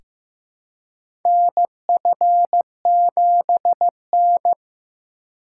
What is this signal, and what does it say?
Morse code, heard as a keyed tone, about 15 wpm, 700 Hz. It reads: NF7N
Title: USA Call Signs - 4 Characters - 4x Character Spacing - 15wpm